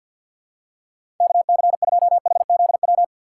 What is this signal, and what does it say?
Morse code, heard as a keyed tone, about 47 wpm, 700 Hz. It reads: KC2HZW
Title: USA Call Signs - 2x Character Spacing - 50wpm